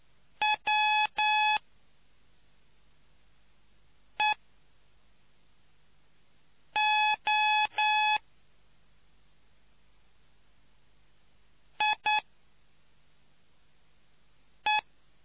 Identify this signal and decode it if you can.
Morse code, heard as a keyed tone, about 9 wpm, spaced as for 3 wpm, 850 Hz. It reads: WEO IE